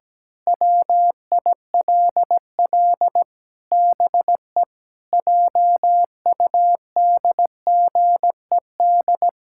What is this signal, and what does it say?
Morse code, heard as a keyed tone, about 17 wpm, 700 Hz. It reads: WILL BE JUDGED
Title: Pangram Challenge v2 - 17wpm